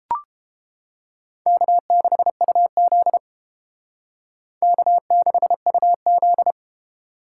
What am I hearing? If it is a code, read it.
Morse code, heard as a keyed tone, about 33 wpm, 700 Hz. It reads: K6UZ K6UZ